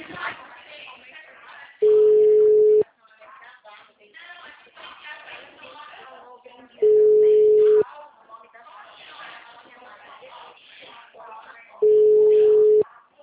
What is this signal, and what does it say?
Ringback tone (Europe, 425 Hz), with speech.
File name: out- -2010- - - .19805